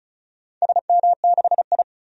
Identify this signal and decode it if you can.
Morse code, heard as a keyed tone, about 34 wpm, 700 Hz. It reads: SM6I
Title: DXCC-100 Call Signs - 4 Characters - 35wpm